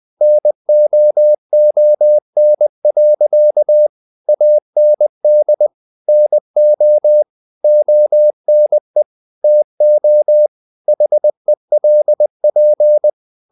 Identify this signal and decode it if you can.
Morse code, heard as a keyed tone, about 20 wpm, 600 Hz. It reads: NOON. AND NO ONE TO HELP
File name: the old man and the sea / the old man and the sea part 1 20wpm